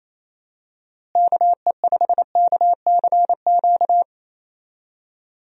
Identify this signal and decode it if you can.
Morse code, heard as a keyed tone, about 28 wpm, 700 Hz. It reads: KE5KCQ
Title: USA Call Signs - 6 Characters - Repeated 2x - 28wpm